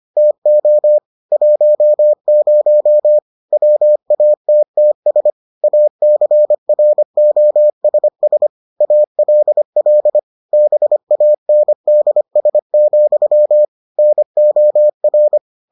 Morse code, heard as a keyed tone, about 25 wpm, 600 Hz. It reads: TO 10 WATTS ACROSS ALL BANDS, NOR